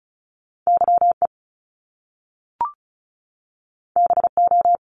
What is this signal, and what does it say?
Morse code, heard as a keyed tone, about 36 wpm, 700 Hz. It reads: YE BO
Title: Top Two Letter Combinations in Words 35wpm